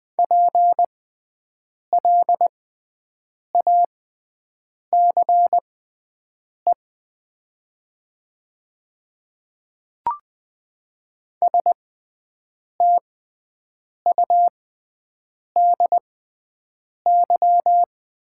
Morse code, heard as a keyed tone, about 21 wpm, spaced as for 7 wpm, 700 Hz. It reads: PLACE STUDY
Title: 6x Character Spacing - Top 300 Words - 20wpm